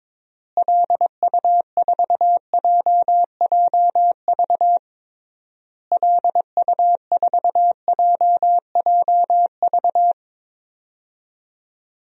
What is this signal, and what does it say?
Morse code, heard as a keyed tone, about 22 wpm, 700 Hz. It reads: LU4JJV LU4JJV
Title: DXCC-100 Call Signs - 6 Characters - Repeated 3x - 22wpm